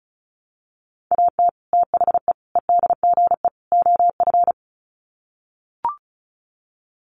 Morse code, heard as a keyed tone, about 34 wpm, 700 Hz. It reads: AT THE EDGE OF
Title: Sets of 4 Words 35wpm